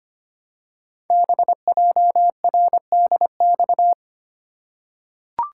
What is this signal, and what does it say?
Morse code, heard as a keyed tone, about 25 wpm, 700 Hz. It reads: BJRDX
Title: Code Groups - 5 Letters - 25wpm